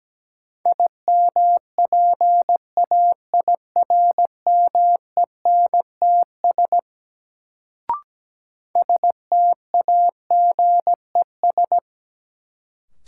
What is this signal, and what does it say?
Morse code, heard as a keyed tone, about 17 wpm, 700 Hz. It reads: IMPAIRMENTS STAGES
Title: Sets of 5 Words Encore - New Word Drill - 17wpm